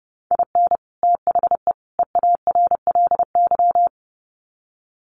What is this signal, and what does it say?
Morse code, heard as a keyed tone, about 29 wpm, 700 Hz. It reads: IN THE EARLY
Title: Sets of 3 Words 30wpm